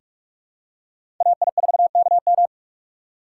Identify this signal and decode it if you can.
Morse code, heard as a keyed tone, about 41 wpm, 700 Hz. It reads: AI4KM
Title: USA Call Signs - 2x Character Spacing - 45wpm